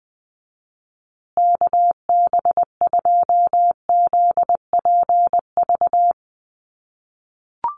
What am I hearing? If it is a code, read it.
Morse code, heard as a keyed tone, about 20 wpm, 700 Hz. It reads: KB2ZPV
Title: USA Call Signs Repeated 2x 20wpm